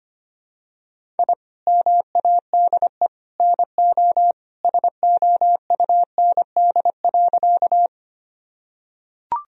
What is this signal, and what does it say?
Morse code, heard as a keyed tone, about 25 wpm, 700 Hz. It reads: I MADE NO SOUND.
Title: Sentences from Top 1000 Words - Review Part 1 - 25wpm